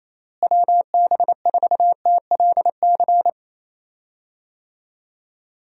Morse code, heard as a keyed tone, about 27 wpm, 700 Hz. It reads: WB4TLC
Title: USA Call Signs - 6 Characters - Repeated 2x - 28wpm